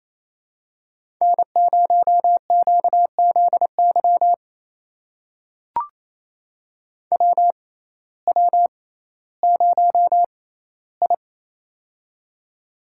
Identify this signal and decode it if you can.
Morse code, heard as a keyed tone, about 28 wpm, spaced as for 15 wpm, 700 Hz. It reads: N0QZY WW0I
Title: USA Call Signs - 6x Character Spacing - 28wpm